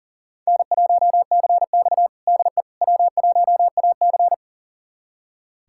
Morse code, heard as a keyed tone, about 40 wpm, 700 Hz. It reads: N1CX DE W1AC
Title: USA Callsign DE Callsign - 4 Characters - 40wpm